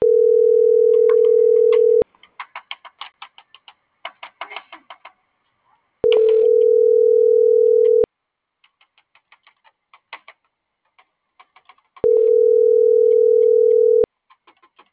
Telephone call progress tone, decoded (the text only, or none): ringback tone